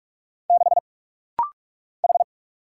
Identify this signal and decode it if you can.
Morse code, heard as a keyed tone, about 44 wpm, 690 Hz. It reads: 6 H
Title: Single Letter-Number - Rapid-Fire 45wpm